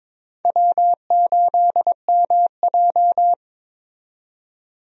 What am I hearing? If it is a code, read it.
Morse code, heard as a keyed tone, about 22 wpm, 700 Hz. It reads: W8MJ
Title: USA Call Signs - 4 Characters - 22wpm